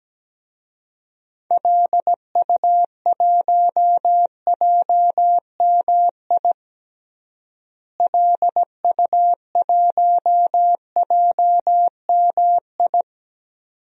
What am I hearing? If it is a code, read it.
Morse code, heard as a keyed tone, about 17 wpm, 700 Hz. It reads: LU1JMI LU1JMI
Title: DXCC-100 Call Signs - 6 Characters - Repeated 2x - 17wpm